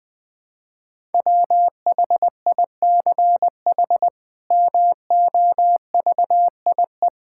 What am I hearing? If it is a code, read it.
Morse code, heard as a keyed tone, about 20 wpm, 700 Hz. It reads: WHICH MOVIE
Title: Sentences from Top 1000 Words - Review Part 5 - 20wpm